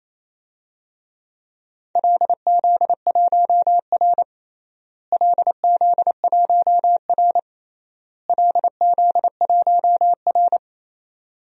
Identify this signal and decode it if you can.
Morse code, heard as a keyed tone, about 27 wpm, 700 Hz. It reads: LZ1R LZ1R LZ1R
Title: DXCC-100 Call Signs - 4 Characters - Repeated 3x - 28wpm